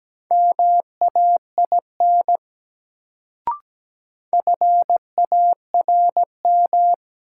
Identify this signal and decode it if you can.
Morse code, heard as a keyed tone, about 17 wpm, 700 Hz. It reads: MAIN FARM
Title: Sentences from Top 1000 Words - Review Part 2 - New Word Drill - 17wpm